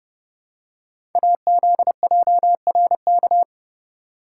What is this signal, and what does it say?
Morse code, heard as a keyed tone, about 29 wpm, 700 Hz. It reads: AZJRK
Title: Code Groups - 5 Letter:Numbers - 30wpm